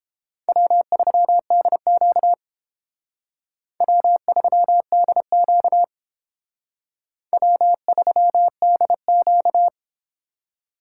Morse code, heard as a keyed tone, about 29 wpm, 700 Hz. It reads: W3DQ W3DQ W3DQ